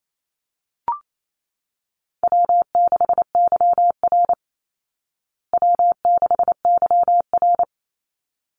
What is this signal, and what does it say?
Morse code, heard as a keyed tone, about 28 wpm, 700 Hz. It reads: W6YR W6YR
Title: Call Signs Repeated 3x - 28wpm